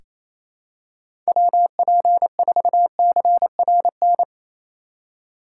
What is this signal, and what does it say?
Morse code, heard as a keyed tone, about 28 wpm, 700 Hz. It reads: WP4CRN
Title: USA Call Signs - 6 Characters - 3x Character Spacing - 28wpm